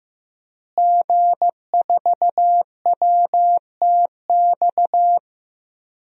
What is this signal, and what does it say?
Morse code, heard as a keyed tone, about 15 wpm, 700 Hz. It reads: G4WTX
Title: DXCC-100 Call Signs - 5 Characters - Repeated 3x - 15wpm